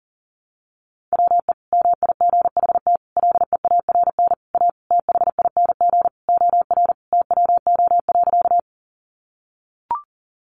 Morse code, heard as a keyed tone, about 40 wpm, 700 Hz. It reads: WE MIGHT LEARN A THING OR TWO.
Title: Sentences from Top 300 Words 40wpm